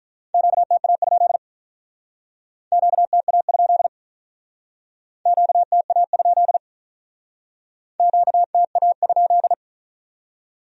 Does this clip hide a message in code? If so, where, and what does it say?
Yes: Morse code from 0.3 to 9.5 s, heard as a keyed tone, about 46 wpm, 700 Hz. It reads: QTA? QTA? QTA? QTA?